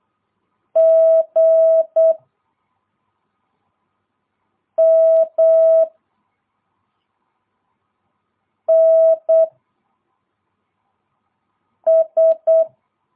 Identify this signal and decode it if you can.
Morse code, heard as a keyed tone, about 8 wpm, 650 Hz. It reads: G M N S
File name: jh8cbh / 2022cw